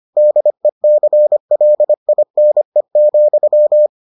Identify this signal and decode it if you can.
Morse code, heard as a keyed tone, about 25 wpm, 600 Hz. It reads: DECLINE,